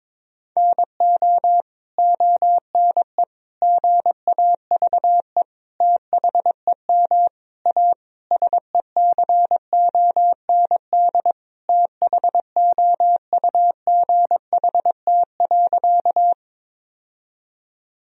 Morse code, heard as a keyed tone, about 22 wpm, 700 Hz. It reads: NO ONE GAVE THEM A SECOND THOUGHT.
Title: Sentences from Top 1000 Words - Review Part 1 - 22wpm